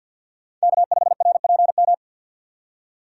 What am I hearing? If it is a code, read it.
Morse code, heard as a keyed tone, about 51 wpm, 700 Hz. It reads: K5RPM
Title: USA Call Signs - 5 Characters - 4x Character Spacing - 50wpm